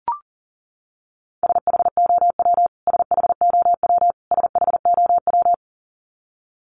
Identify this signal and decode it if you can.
Morse code, heard as a keyed tone, about 40 wpm, 700 Hz. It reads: SHOW SHOW SHOW
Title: Top 500 Words Repeated 40wpm